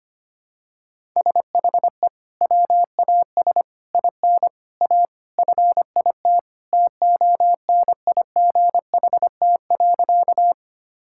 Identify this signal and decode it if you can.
Morse code, heard as a keyed tone, about 25 wpm, 700 Hz. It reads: SHE WAS IN A FIT TONIGHT.
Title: Sentences from Top 1000 Words - Review Part 5 - 25wpm